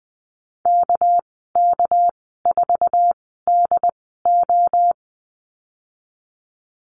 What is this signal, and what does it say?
Morse code, heard as a keyed tone, about 19 wpm, spaced as for 14 wpm, 700 Hz. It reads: KK4DO
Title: 200 Lesson 32g - Letter k - Call Signs 20wpm